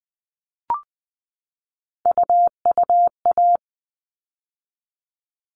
Morse code, heard as a keyed tone, about 20 wpm, 700 Hz. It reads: UUA
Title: All Cut Numbers - 3-Digit 20wpm